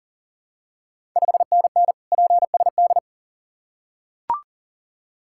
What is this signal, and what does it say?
Morse code, heard as a keyed tone, about 40 wpm, 690 Hz. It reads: 5NN PSD
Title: Ontario QSO Party Exchange 40wpm